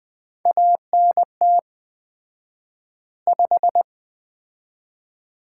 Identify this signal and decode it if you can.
Morse code, heard as a keyed tone, about 20 wpm, 700 Hz. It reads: ANT 5
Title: Common QSO Phrases - 4x Word-Spacing - 20wpm